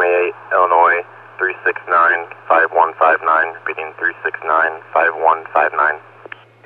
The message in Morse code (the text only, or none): none